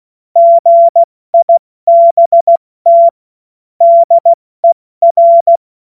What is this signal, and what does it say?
Morse code, heard as a keyed tone, about 16 wpm, 680 Hz. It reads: GIBT DER
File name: Morsekurs / Witze / Witz 11. April 70 F80